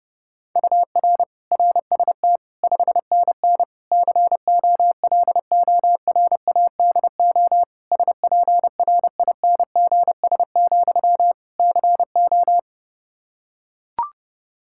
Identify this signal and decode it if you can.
Morse code, heard as a keyed tone, about 30 wpm, 700 Hz. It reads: UR RST 5NN COLORADO SPRINGS, CO
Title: Practice QSO - Exchange 1 - Beginning - 30wpm